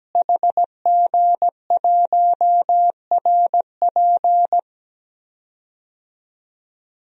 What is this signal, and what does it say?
Morse code, heard as a keyed tone, about 17 wpm, 700 Hz. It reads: HG1RP
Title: DXCC-100 Call Signs - 5 Characters - Repeated 3x - 17wpm